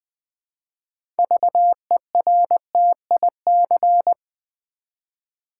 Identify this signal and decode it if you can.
Morse code, heard as a keyed tone, about 20 wpm, 700 Hz. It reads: VERTIC
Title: Build A Word - 2 Letters At A Time - 8-Letter Words - 20wpm